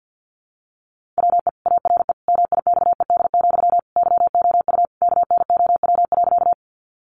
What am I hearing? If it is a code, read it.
Morse code, heard as a keyed tone, about 50 wpm, 700 Hz. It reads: WE ARE MIXED, YOU KNOW.
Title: Sentences from Top 1500 Words 50wpm